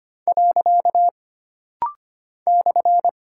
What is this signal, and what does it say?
Morse code, heard as a keyed tone, about 25 wpm, 700 Hz. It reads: . /
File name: Single Punctuation-Prosign - Rapid-Fire - 25wpm